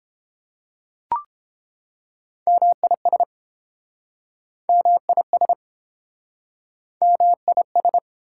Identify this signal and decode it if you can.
Morse code, heard as a keyed tone, about 30 wpm, 700 Hz. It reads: MIS MIS MIS